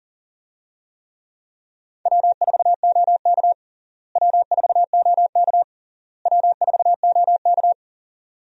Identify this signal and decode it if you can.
Morse code, heard as a keyed tone, about 40 wpm, 700 Hz. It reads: W4OK W4OK W4OK
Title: USA Call Signs - 4 Characters - Repeated 3x - 40wpm